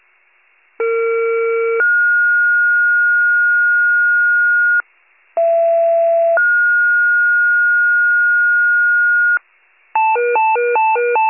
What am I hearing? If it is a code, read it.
Two-tone page, heard as a one-second tone, then a three-second tone, about 460 then 1430 Hz, about 670 then 1430 Hz. It is heard from 0.8 to 9.4 s.
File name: pager / 39 FIRE / wcalert